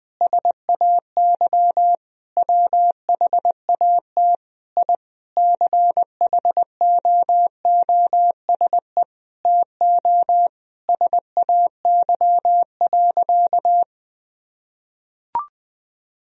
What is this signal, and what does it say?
Morse code, heard as a keyed tone, about 20 wpm, 700 Hz. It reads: SAY WHAT I CHOOSE TO SAY.
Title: Sentences from Top 1000 Words - Review Part 4 - 20wpm